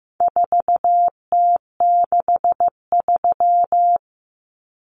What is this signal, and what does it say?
Morse code, heard as a keyed tone, about 15 wpm, 700 Hz. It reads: 4T63
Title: Common Cut Numbers - 4-Digit 15wpm